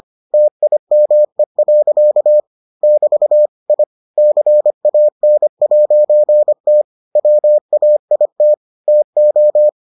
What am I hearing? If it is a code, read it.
Morse code, heard as a keyed tone, about 25 wpm, 600 Hz. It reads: TIME. <BT> I CAN'T WAIT TO